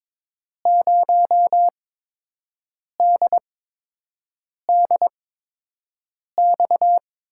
Morse code, heard as a keyed tone, about 22 wpm, spaced as for 6 wpm, 700 Hz. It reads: 0DDX
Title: USA Call Signs - 6 Characters - 8x Character Spacing - 22wpm